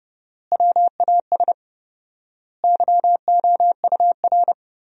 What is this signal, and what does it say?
Morse code, heard as a keyed tone, about 30 wpm, 700 Hz. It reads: WAS YOUR
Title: 4x Word-Spacing - Sentences from Top 200 Words - 30wpm